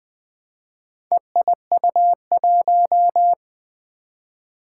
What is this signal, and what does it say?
Morse code, heard as a keyed tone, about 20 wpm, 700 Hz. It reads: EIU1
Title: Four Challenging Letter-Number Combinations 20wpm